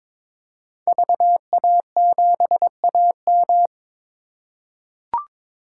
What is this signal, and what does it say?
Morse code, heard as a keyed tone, about 22 wpm, 700 Hz. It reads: VA7AM
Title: K1USN SST QSO Practice - 22wpm